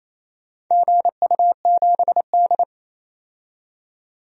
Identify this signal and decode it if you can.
Morse code, heard as a keyed tone, about 28 wpm, 700 Hz. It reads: GU7D
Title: DXCC-100 Call Signs - 4 Characters - 28wpm